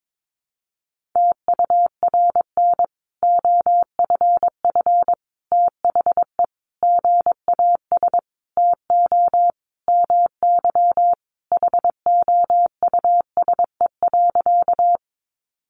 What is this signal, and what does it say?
Morse code, heard as a keyed tone, about 22 wpm, 700 Hz. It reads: TURN OFF THE GAS TO MY HOUSE.
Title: Sentences from Top 1100 Words 22wpm